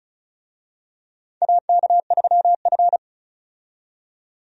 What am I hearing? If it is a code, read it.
Morse code, heard as a keyed tone, about 34 wpm, 700 Hz. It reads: AK3F
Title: USA Call Signs - 4 Characters - 35wpm